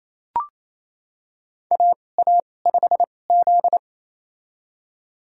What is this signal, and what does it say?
Morse code, heard as a keyed tone, about 28 wpm, spaced as for 20 wpm, 700 Hz. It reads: AA5Z
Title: USA Call Signs - 4 Characters - 2x Character Spacing - 28wpm